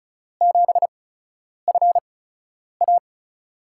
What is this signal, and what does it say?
Morse code, heard as a keyed tone, about 36 wpm, spaced as for 10 wpm, 700 Hz. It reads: 7FA
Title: USA Call Signs - 8x Character Spacing - 35wpm